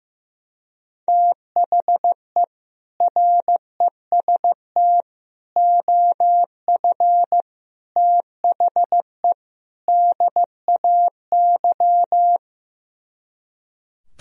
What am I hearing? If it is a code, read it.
Morse code, heard as a keyed tone, about 15 wpm, 700 Hz. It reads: THE REST OF THE DAY
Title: Speed-Racing - Sets of 5 Words 15wpm